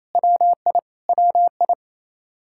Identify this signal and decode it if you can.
Morse code, heard as a keyed tone, about 28 wpm, 700 Hz. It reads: WI WI
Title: K1USN SST QSO Practice 28wpm